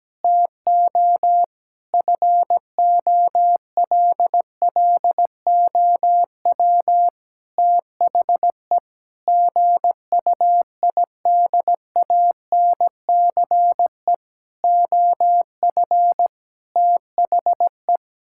Morse code, heard as a keyed tone, about 17 wpm, 700 Hz. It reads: TO FOLLOW THE GUIDANCE OF THE
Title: Sentences from Top 1600 Words 17wpm